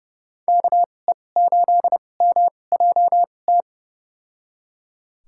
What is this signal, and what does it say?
Morse code, heard as a keyed tone, about 32 wpm, spaced as for 22 wpm, 700 Hz. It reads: KE8MJT
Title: USA Call Signs - 6 Characters - 2x Character Spacing - 30wpm